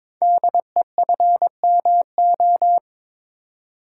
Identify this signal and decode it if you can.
Morse code, heard as a keyed tone, about 22 wpm, 700 Hz. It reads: DEFMO
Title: Code Groups - 5 Letters - 22wpm